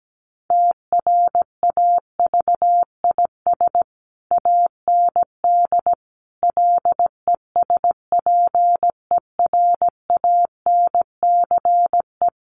Morse code, heard as a keyed tone, about 17 wpm, 700 Hz. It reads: TRAVIS AND LESPERANCE
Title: A Sound Of Thunder 17wpm